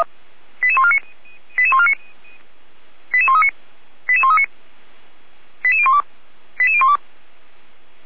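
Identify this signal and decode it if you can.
Five-tone selective call (ZVEI), heard as quick runs of five tones, tones of about 80 ms. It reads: 88149 88139 88139 88012 88012
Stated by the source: Five-tone ZVEI selective call, code 88122, received by SDR.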